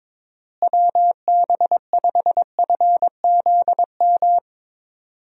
Morse code, heard as a keyed tone, about 22 wpm, 700 Hz. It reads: WB5FZM
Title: USA Call Signs - 6 Characters - Repeated 3x - 22wpm